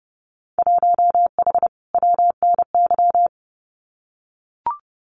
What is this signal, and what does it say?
Morse code, heard as a keyed tone, about 28 wpm, 700 Hz. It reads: 1H WNY
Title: Winter Field Day Exchanges Encore 30wpm